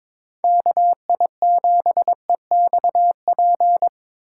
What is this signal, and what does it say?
Morse code, heard as a keyed tone, about 26 wpm, 700 Hz. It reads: KI7EXP
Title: Speed Racing - USA 2x3 Call Signs 22wpm